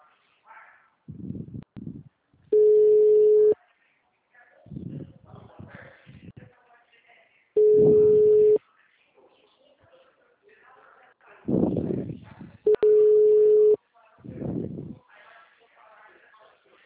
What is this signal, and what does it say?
Ringback tone (Europe, 425 Hz), with speech.
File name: out- -2048- - - .77890